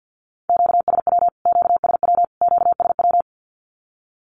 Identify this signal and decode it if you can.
Morse code, heard as a keyed tone, about 50 wpm, 700 Hz. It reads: QSW QSW QSW
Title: Q-Codes Repeated 3x 50wpm